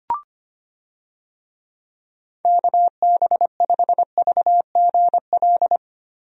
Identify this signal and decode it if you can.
Morse code, heard as a keyed tone, about 25 wpm, 700 Hz. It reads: KB5VGL
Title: USA Call Signs - 6 Characters - Repeated 3x - 25wpm